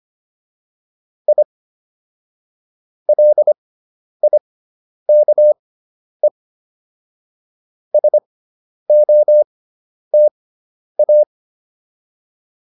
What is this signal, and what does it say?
Morse code, heard as a keyed tone, about 25 wpm, spaced as for 10 wpm, 600 Hz. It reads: I LIKE SOTA